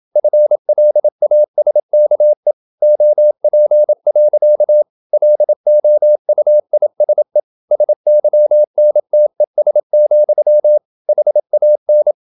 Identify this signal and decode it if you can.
Morse code, heard as a keyed tone, about 27 wpm, 600 Hz. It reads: FLASKE OP. LOUISE SYNTES, HAN
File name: Ferie QRQ 27wpm